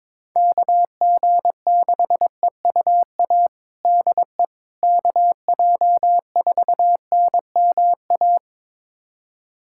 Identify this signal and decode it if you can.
Morse code, heard as a keyed tone, about 21 wpm, 700 Hz. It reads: KG6EUA DE KJ4NMA
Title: USA Callsign DE Callsign - 6 Characters - 22wpm